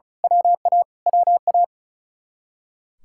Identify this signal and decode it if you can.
Morse code, heard as a keyed tone, about 33 wpm, 700 Hz. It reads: WA WA
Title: Speed-Racing Common QSO Phrases - 35wpm